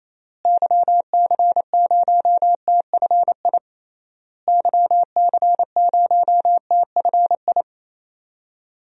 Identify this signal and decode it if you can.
Morse code, heard as a keyed tone, about 28 wpm, 700 Hz. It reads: YC0TFI YC0TFI
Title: DXCC-100 Call Signs - 6 Characters - Repeated 3x - 28wpm